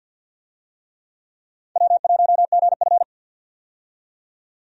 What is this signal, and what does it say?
Morse code, heard as a keyed tone, about 51 wpm, 700 Hz. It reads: W1GF
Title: USA Call Signs - 4 Characters - 50wpm